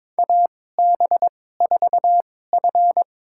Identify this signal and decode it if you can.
Morse code, heard as a keyed tone, about 22 wpm, spaced as for 16 wpm, 700 Hz. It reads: AB4F
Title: USA Call Signs - 4 Characters - 2x Character Spacing - 22wpm